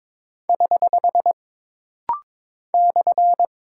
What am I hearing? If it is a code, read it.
Morse code, heard as a keyed tone, about 22 wpm, 700 Hz. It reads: <HH> /
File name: practice / 22wpm / Single Punctuation-Prosign - Rapid-Fire - 22wpm